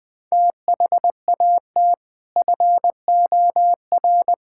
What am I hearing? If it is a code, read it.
Morse code, heard as a keyed tone, about 20 wpm, 700 Hz. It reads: THAT FOR
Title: Sentences from Top 1000 Words - Review Part 5 - 20wpm